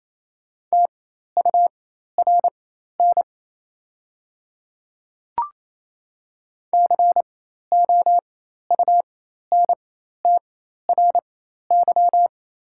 Morse code, heard as a keyed tone, about 28 wpm, spaced as for 13 wpm, 690 Hz. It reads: TURN COUNTRY
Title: 4x Character Spacing - Top 300 Words - 28wpm